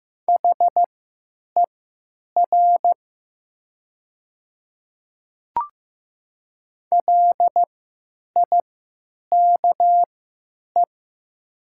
Morse code, heard as a keyed tone, about 15 wpm, spaced as for 9 wpm, 700 Hz. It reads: HER LIKE
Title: 3x Character Spacing - Top 100 Words - 15wpm